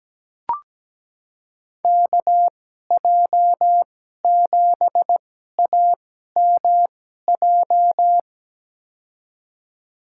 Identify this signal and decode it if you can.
Morse code, heard as a keyed tone, about 17 wpm, spaced as for 12 wpm, 700 Hz. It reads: KJ7AMJ
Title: USA Call Signs - 6 Characters - 2x Character Spacing - 17wpm